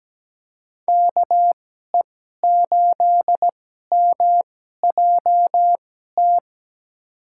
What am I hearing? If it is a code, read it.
Morse code, heard as a keyed tone, about 17 wpm, spaced as for 12 wpm, 700 Hz. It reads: KE8MJT
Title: USA Call Signs - 2x Character Spacing - 17wpm